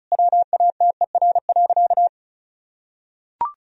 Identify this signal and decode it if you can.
Morse code, heard as a keyed tone, about 36 wpm, 700 Hz. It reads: WATER.
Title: Sentences from Top 1000 Words - Review Part 1 - 35wpm